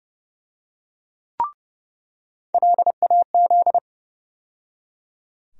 Morse code, heard as a keyed tone, about 30 wpm, 700 Hz. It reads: LAZ
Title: USA Call Sign Postfixes - 3 Letters 30wpm